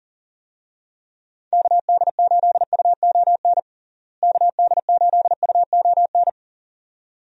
Morse code, heard as a keyed tone, about 40 wpm, 700 Hz. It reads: KD8UON KD8UON
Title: USA Call Signs - 6 Characters - Repeated 2x - 40wpm